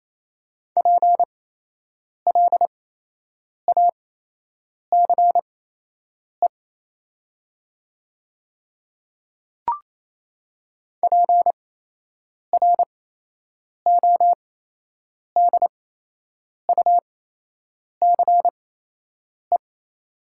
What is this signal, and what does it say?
Morse code, heard as a keyed tone, about 28 wpm, spaced as for 8 wpm, 700 Hz. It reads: PLACE PRODUCE
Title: 8x Character Spacing - Top 500 Words - 28wpm